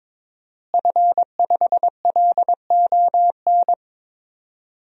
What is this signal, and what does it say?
Morse code, heard as a keyed tone, about 22 wpm, 700 Hz. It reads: F5LON
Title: DXCC-100 Call Signs - 5 Characters - Repeated 2x - 22wpm